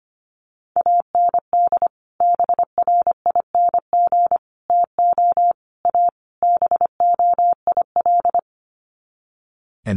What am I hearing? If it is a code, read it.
Morse code, heard as a keyed tone, about 25 wpm, 700 Hz. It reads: AND BRING TO A BOIL
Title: Sets of 5 Words Encore 25wpm